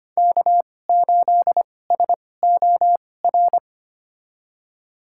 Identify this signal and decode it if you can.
Morse code, heard as a keyed tone, about 25 wpm, spaced as for 18 wpm, 700 Hz. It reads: K8SOR
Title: USA Call Signs - 5 Characters - 2x Character Spacing - 25wpm